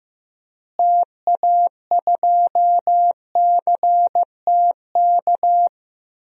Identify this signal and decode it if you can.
Morse code, heard as a keyed tone, about 16 wpm, 700 Hz. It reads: TA2CTK
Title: DXCC-100 Call Signs - 6 Characters - Repeated 3x - 15wpm